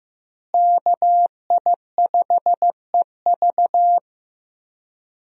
Morse code, heard as a keyed tone, about 15 wpm, 700 Hz. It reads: KI5EV
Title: USA Call Signs - 5 Characters - Repeated 3x - 15wpm